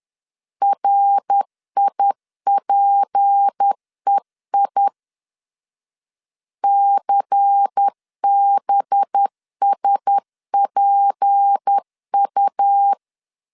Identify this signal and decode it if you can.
Morse code, heard as a keyed tone, about 10 wpm, 800 Hz. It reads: RIPEI CBSPU
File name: LEZIONI / 6 PKX0 / 6 P 10wpm 2432